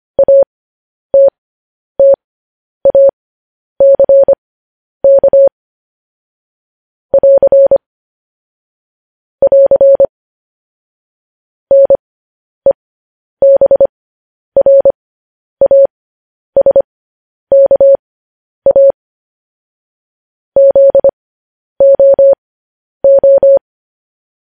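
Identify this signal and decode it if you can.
Morse code, heard as a keyed tone, about 25 wpm, spaced as for 10 wpm, 550 Hz. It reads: ATTACK <AR> <AR> NEBRASKA ZOO
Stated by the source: Morse code news headlines sent at 10 wpm.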